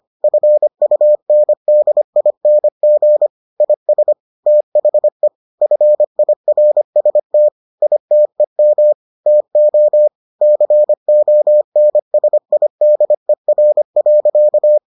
Morse code, heard as a keyed tone, about 25 wpm, 600 Hz. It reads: FUNDING IS THE FIRST ITEM TO CONSIDER.